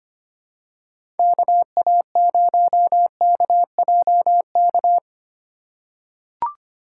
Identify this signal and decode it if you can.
Morse code, heard as a keyed tone, about 25 wpm, 700 Hz. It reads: KA0KJK
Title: USA Call Signs - 6 Characters - 3x Character Spacing - 25wpm